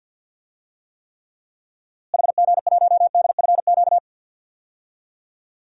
Morse code, heard as a keyed tone, about 52 wpm, 700 Hz. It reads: HG1DFQ
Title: DXCC-100 Call Signs - 6 Characters - Repeated 3x - 50wpm